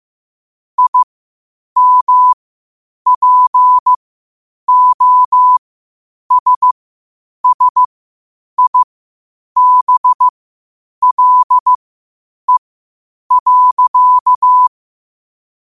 Morse code, heard as a keyed tone, about 15 wpm, spaced as for 8 wpm, 1000 Hz. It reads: IMPOSSIBLE.
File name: qotdfiles / 8wpm